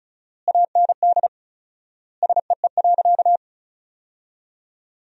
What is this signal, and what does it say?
Morse code, heard as a keyed tone, about 36 wpm, 700 Hz. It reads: AND SEE.
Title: 4x Word-Spacing - Sentences from Top 100 Words - 35wpm